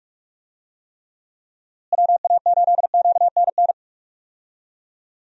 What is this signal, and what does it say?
Morse code, heard as a keyed tone, about 46 wpm, 700 Hz. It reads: WA8QNN
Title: USA Call Signs - 6 Characters - Repeated 3x - 45wpm